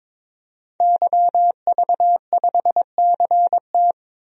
Morse code, heard as a keyed tone, about 22 wpm, 700 Hz. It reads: YV5CT
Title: DXCC-100 Call Signs - 5 Characters - 22wpm